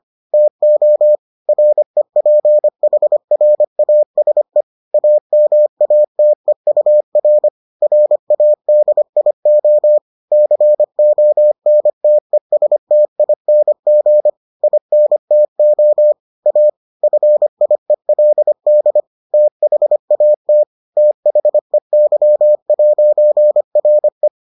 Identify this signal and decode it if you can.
Morse code, heard as a keyed tone, about 25 wpm, 600 Hz. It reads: TO REPHRASE AMATEUR RADIO CONTESTING INTO A FIELD THAT THEY'RE